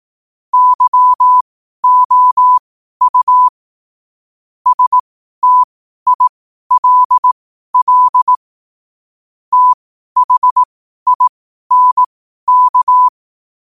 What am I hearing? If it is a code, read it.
Morse code, heard as a keyed tone, about 18 wpm, spaced as for 12 wpm, 1000 Hz. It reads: YOU STILL THINK
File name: qotdfiles / 12wpm